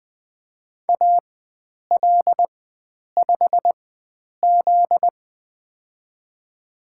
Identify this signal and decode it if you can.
Morse code, heard as a keyed tone, about 20 wpm, spaced as for 9 wpm, 700 Hz. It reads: AL5Z
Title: USA Call Signs - 4x Character Spacing - 20wpm